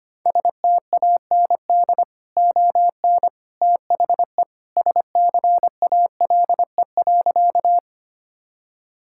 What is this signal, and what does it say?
Morse code, heard as a keyed tone, about 25 wpm, 700 Hz. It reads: STAND ON THE SCALE.
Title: Sentences from Top 1000 Words - Review Part 5 - 25wpm